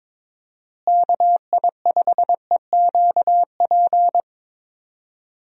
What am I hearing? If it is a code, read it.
Morse code, heard as a keyed tone, about 22 wpm, 700 Hz. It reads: KI5EQP
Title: USA Call Signs - 6 Characters - Repeated 3x - 22wpm